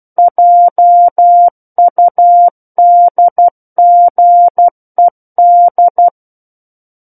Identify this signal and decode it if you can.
Morse code, heard as a keyed tone, about 12 wpm, 700 Hz. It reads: JUDGED
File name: Book One / 03-Chapter 3 - 12 WPM